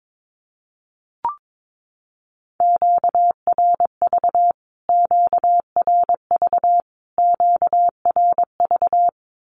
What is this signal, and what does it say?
Morse code, heard as a keyed tone, about 22 wpm, 700 Hz. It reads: QRV QRV QRV
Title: Q-Codes Repeated 3x 22wpm